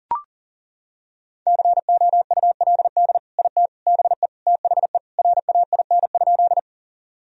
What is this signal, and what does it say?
Morse code, heard as a keyed tone, about 40 wpm, 700 Hz. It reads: COULD IT BE THE RAIN?
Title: Sentences from Top 1000 Words - Review Part 1 - 40wpm